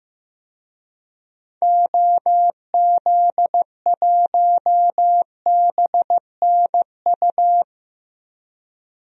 Morse code, heard as a keyed tone, about 16 wpm, 700 Hz. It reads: OZ1BNU
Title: DXCC-100 Call Signs - 6 Characters - Repeated 2x - 15wpm